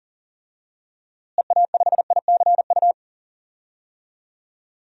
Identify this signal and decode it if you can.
Morse code, heard as a keyed tone, about 40 wpm, 700 Hz. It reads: EA5ICU
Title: DXCC-100 Call Signs - 6 Characters - Repeated 3x - 40wpm